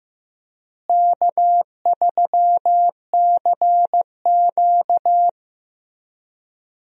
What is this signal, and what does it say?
Morse code, heard as a keyed tone, about 15 wpm, 700 Hz. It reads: K3CQ
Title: USA Call Signs - 4 Characters - Repeated 3x - 15wpm